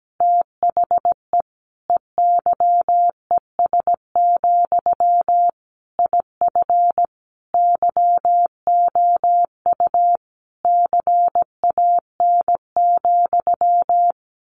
Morse code, heard as a keyed tone, about 17 wpm, 700 Hz. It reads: THE EYES, IF YOU CAN,
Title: A Sound Of Thunder 17wpm